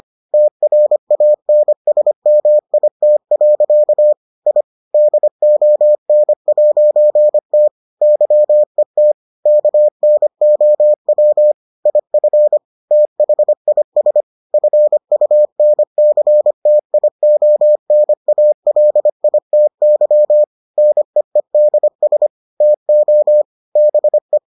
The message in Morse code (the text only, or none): TRANSMIT. I DON'T YET KNOW IF THIS FUNCTIONALITY NEEDS TO BE